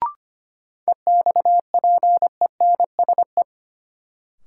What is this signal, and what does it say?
Morse code, heard as a keyed tone, about 25 wpm, 700 Hz. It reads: EXPENSE
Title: Sets of 4 Words - New Word Drill - 25wpm